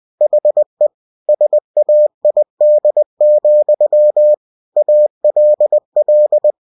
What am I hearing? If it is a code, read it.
Morse code, heard as a keyed tone, about 20 wpm, 600 Hz. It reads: HE SAID, ALL
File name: cw-books / winnie-the-pooh / winnie-20wpm-page-0037